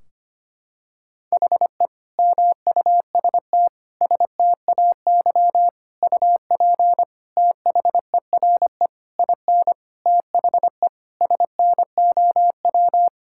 Morse code, heard as a keyed tone, about 25 wpm, 700 Hz. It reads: HE MUST STAY UP THERE IN THE SNOW
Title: Sentences from Top 1600 Words 25wpm